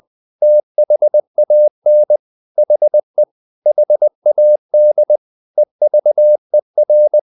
Morse code, heard as a keyed tone, about 20 wpm, 600 Hz. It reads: THAN HE HAD EVER
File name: the old man and the sea / the old man and the sea part 1 20wpm